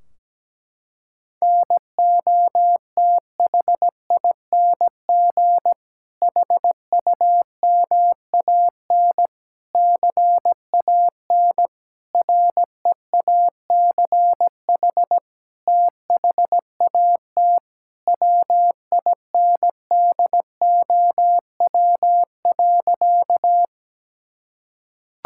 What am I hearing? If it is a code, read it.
Morse code, heard as a keyed tone, about 17 wpm, 700 Hz. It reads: NOTHING HUMAN CAN REACH THAT WINDOW.
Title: Sentences from Top 1600 Words 17wpm